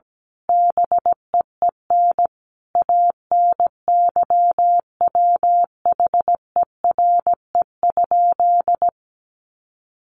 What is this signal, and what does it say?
Morse code, heard as a keyed tone, about 17 wpm, 700 Hz. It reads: BEEN ANYWHERE?
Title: Sentences from Top 1400 Words 17wpm